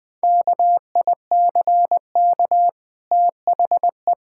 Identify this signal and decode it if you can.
Morse code, heard as a keyed tone, about 20 wpm, 700 Hz. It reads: KICK THE
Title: Easy Sentences with Hints - 20wpm